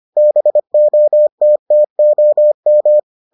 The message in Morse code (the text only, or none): BOTTOM